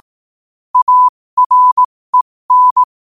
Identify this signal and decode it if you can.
Morse code, heard as a keyed tone, about 18 wpm, spaced as for 15 wpm, 1000 Hz. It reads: AREN
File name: qotdfiles / 15wpm